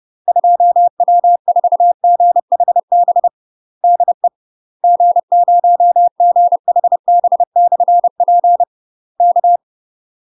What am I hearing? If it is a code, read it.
Morse code, heard as a keyed tone, about 30 wpm, 700 Hz. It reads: 2W4GHB DE G0GHB/P K
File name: UK MASTER QSOs TRAINING / 30WPM / UK MASTER QSOs 0038